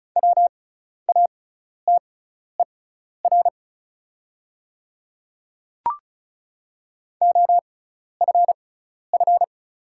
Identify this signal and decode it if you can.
Morse code, heard as a keyed tone, about 35 wpm, spaced as for 12 wpm, 690 Hz. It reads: WATER OFF
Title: 6x Character Spacing - Top 300 Words - 35wpm